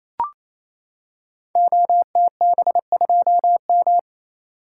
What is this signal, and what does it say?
Morse code, heard as a keyed tone, about 28 wpm, 700 Hz. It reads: OTB2M
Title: Code Groups - 5 Letter:Numbers - 28wpm